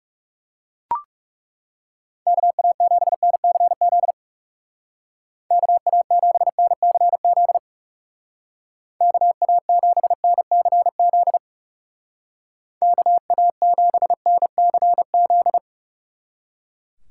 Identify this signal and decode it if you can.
Morse code, heard as a keyed tone, about 35 wpm, 700 Hz. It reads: KA7NCZ KA7NCZ KA7NCZ KA7NCZ
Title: Speed Racing - USA 2x3 Call Signs 30wpm